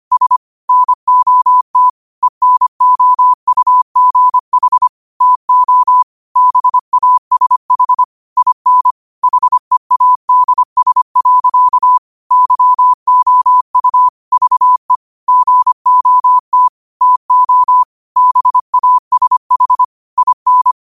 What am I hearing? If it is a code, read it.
Morse code, heard as a keyed tone, about 25 wpm, 1000 Hz. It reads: S NOT ENOUGH TO BASH IN HEADS. YOU VE GOT TO BASH IN
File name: qotdfiles / 25wpm